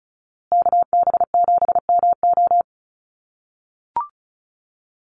Morse code, heard as a keyed tone, about 34 wpm, 700 Hz. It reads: KB7MO
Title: USA Call Signs Repeated 2x 35wpm